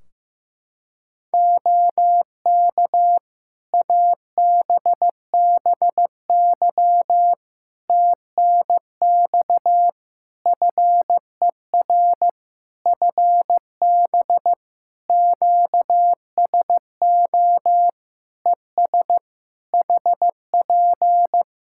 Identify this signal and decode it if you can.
Morse code, heard as a keyed tone, about 15 wpm, 700 Hz. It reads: OK ABBY TNX FER FB QSO ES HP
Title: Practice QSO - Exchange 4 - Ending - 15wpm